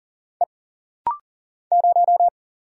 Morse code, heard as a keyed tone, about 40 wpm, 700 Hz. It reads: E 0
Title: Single Letter-Number - Rapid-Fire 40wpm